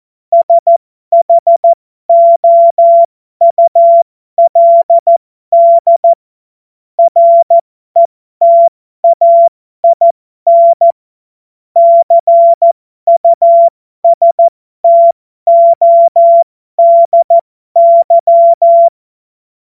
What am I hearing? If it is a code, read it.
Morse code, heard as a keyed tone, about 14 wpm, spaced as for 12 wpm, 680 Hz. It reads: SHOULD RETAIN CUSTODY